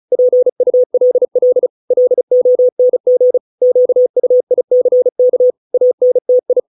Morse code, heard as a keyed tone, about 34 wpm, 500 Hz. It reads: PULL LONG QUICK ANTI